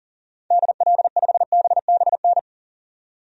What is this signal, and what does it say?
Morse code, heard as a keyed tone, about 40 wpm, 700 Hz. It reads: DL5BBN
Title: DXCC-100 Call Signs - 6 Characters - 40wpm